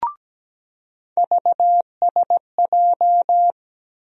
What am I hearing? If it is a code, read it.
Morse code, heard as a keyed tone, about 17 wpm, 700 Hz. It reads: VSJ
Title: USA Call Sign Postfixes - 3 Letters 17wpm